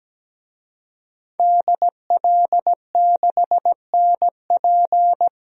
Morse code, heard as a keyed tone, about 17 wpm, 700 Hz. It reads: DL6NP
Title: DXCC-100 Call Signs - 5 Characters - 17wpm